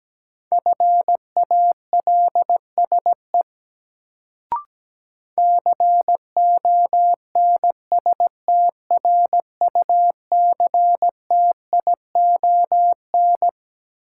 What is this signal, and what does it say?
Morse code, heard as a keyed tone, about 17 wpm, 700 Hz. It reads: FALSE CONSTRUCTION
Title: Sentences from Top 1600 Words - New Word Drill - 17wpm